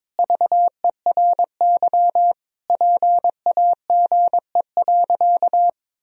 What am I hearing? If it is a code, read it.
Morse code, heard as a keyed tone, about 21 wpm, 700 Hz. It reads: VERY PAGE.
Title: Sentences from Top 1000 Words - Review Part 1 - 22wpm